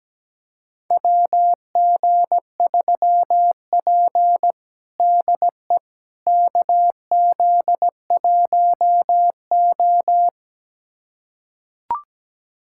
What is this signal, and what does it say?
Morse code, heard as a keyed tone, about 17 wpm, 700 Hz. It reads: WG3P DE KZ1O
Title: USA Callsign DE Callsign - 4 Characters - 17wpm